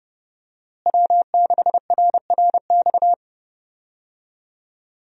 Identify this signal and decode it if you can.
Morse code, heard as a keyed tone, about 32 wpm, 700 Hz. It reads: W6RRX
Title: USA Call Signs - 5 Characters - Repeated 2x - 30wpm